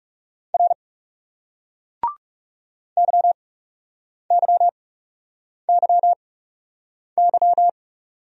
Morse code, heard as a keyed tone, about 39 wpm, 700 Hz. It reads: R Y Y Y Y